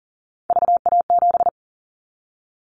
Morse code, heard as a keyed tone, about 43 wpm, 700 Hz. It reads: VA7
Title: Callsign Prefixes for Top 50 DXCC - 40wpm